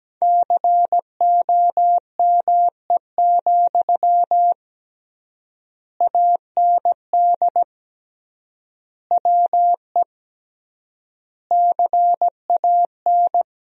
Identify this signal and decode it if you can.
Morse code, heard as a keyed tone, about 17 wpm, 700 Hz. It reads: COME, AND WE CAN
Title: 3x Word-Spacing - Sentences from Top 500 Words - 17wpm